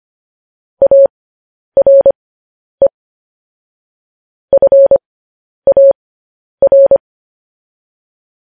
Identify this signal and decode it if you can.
Morse code, heard as a keyed tone, about 25 wpm, spaced as for 10 wpm, 550 Hz. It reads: ARE FAR